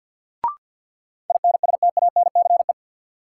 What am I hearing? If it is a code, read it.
Morse code, heard as a keyed tone, about 53 wpm, 700 Hz. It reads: INSTANCE
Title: Sentences from Top 800 Words - New Word Drill - 50wpm